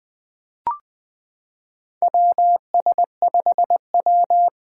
Morse code, heard as a keyed tone, about 20 wpm, 700 Hz. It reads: WS5W
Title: Four Challenging Letter-Number Combinations 20wpm